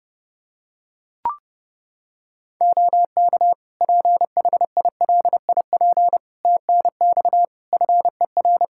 Morse code, heard as a keyed tone, about 30 wpm, 700 Hz. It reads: OK PHILIP TNX FER
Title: Practice QSO - Exchange 4 - Ending - 30wpm